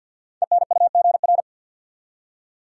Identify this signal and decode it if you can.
Morse code, heard as a keyed tone, about 50 wpm, 700 Hz. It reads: ENUGR
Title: Code Groups - 5 Letters - 50wpm